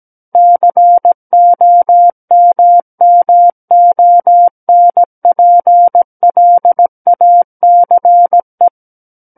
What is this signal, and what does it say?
Morse code, heard as a keyed tone, about 17 wpm, 700 Hz. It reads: COMMONPLACE
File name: Book One / 08-Chapter 8 - 17 WPM